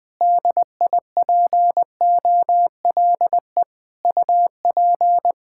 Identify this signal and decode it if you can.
Morse code, heard as a keyed tone, about 21 wpm, 700 Hz. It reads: DIPOLE UP
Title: Practice QSO - Exchange 2 - Info - 20wpm